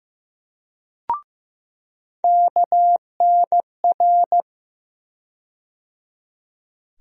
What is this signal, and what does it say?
Morse code, heard as a keyed tone, about 15 wpm, 700 Hz. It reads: KNR
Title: USA Call Sign Postfixes - 3 Letters 15wpm